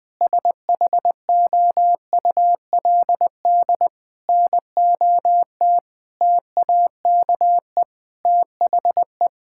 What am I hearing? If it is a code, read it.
Morse code, heard as a keyed tone, about 20 wpm, 700 Hz. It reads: SHOULD NOT TAKE THE
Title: Sentences with 8-Word Phrases - Spoken Beginning - 20wpm